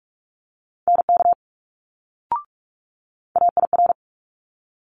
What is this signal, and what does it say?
Morse code, heard as a keyed tone, about 46 wpm, 700 Hz. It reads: NK AIR
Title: Multi-letter Phonemes Quick Fire 45wpm